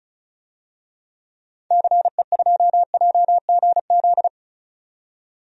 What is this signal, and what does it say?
Morse code, heard as a keyed tone, about 34 wpm, 700 Hz. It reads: CE2JGZ
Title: DXCC-100 Call Signs - 6 Characters - 35wpm